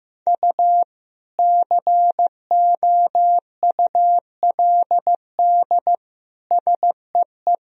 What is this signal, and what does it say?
Morse code, heard as a keyed tone, about 15 wpm, 700 Hz. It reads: U COULD SEE
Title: Sentences from Top 500 Words and Abbreviations 15wpm